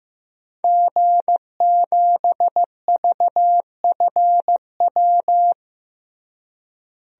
Morse code, heard as a keyed tone, about 15 wpm, 700 Hz. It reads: G7VFW
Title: DXCC-100 Call Signs - 5 Characters - Repeated 3x - 15wpm